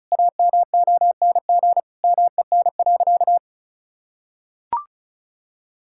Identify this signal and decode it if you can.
Morse code, heard as a keyed tone, about 36 wpm, 700 Hz. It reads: AMONG MEN.
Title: Sentences from Top 1600 Words 35wpm